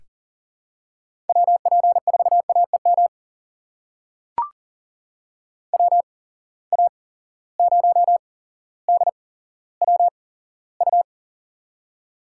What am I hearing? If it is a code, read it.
Morse code, heard as a keyed tone, about 40 wpm, spaced as for 17 wpm, 700 Hz. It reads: WP4AEM WA0DWU
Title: USA Call Signs - 6 Characters - 8x Character Spacing - 40wpm